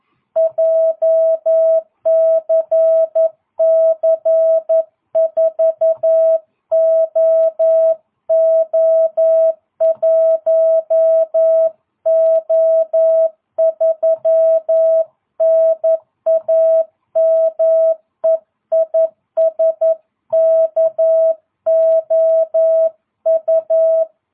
Morse code, heard as a keyed tone, about 11 wpm, 650 Hz. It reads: JCC4OO1O3 NAME IS KOU